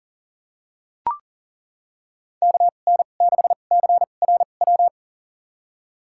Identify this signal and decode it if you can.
Morse code, heard as a keyed tone, about 40 wpm, spaced as for 29 wpm, 700 Hz. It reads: KN6CRW
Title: USA Call Signs - 6 Characters - 2x Character Spacing - 40wpm